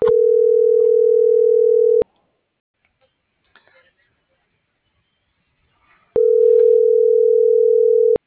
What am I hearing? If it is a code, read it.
Ringback tone (North America), 440 + 480 Hz.